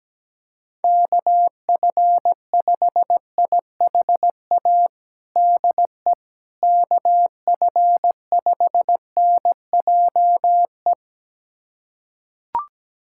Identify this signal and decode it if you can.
Morse code, heard as a keyed tone, about 17 wpm, 700 Hz. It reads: KF5IHA DE KF5NJE
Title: USA Callsign DE Callsign 17wpm